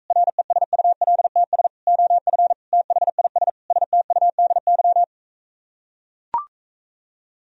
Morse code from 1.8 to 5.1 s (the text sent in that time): OF THIS STUDY